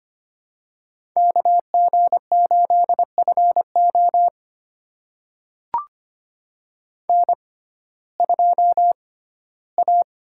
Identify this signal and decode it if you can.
Morse code, heard as a keyed tone, about 25 wpm, spaced as for 15 wpm, 700 Hz. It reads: KG8FO N2A
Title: USA Call Signs - 5 Characters - 6x Character Spacing - 25wpm